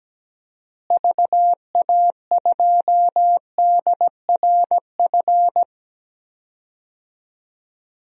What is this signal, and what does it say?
Morse code, heard as a keyed tone, about 17 wpm, 700 Hz. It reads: VA2DRF
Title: DXCC-100 Call Signs - 6 Characters - Repeated 3x - 17wpm